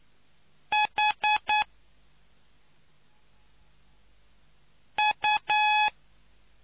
Morse code, heard as a keyed tone, about 9 wpm, 850 Hz. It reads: H U